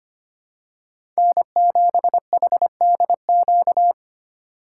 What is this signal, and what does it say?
Morse code, heard as a keyed tone, about 25 wpm, 700 Hz. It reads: N7HDQ
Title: USA Call Signs - 5 Characters - 8x Character Spacing - 25wpm